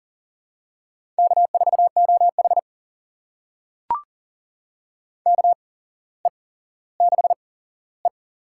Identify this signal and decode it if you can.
Morse code, heard as a keyed tone, about 41 wpm, spaced as for 17 wpm, 700 Hz. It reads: K4OH KE6E
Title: USA Call Signs - 4 Characters - 8x Character Spacing - 40wpm